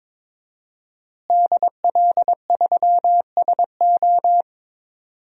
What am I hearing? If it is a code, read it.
Morse code, heard as a keyed tone, about 22 wpm, 700 Hz. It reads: DL3SO
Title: DXCC-100 Call Signs - 5 Characters - Repeated 3x - 22wpm